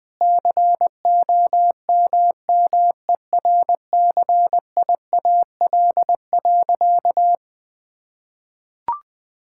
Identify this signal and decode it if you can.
Morse code, heard as a keyed tone, about 20 wpm, 700 Hz. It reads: COMMERCIAL.
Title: Sentences from Top 1000 Words - Review Part 5 - 20wpm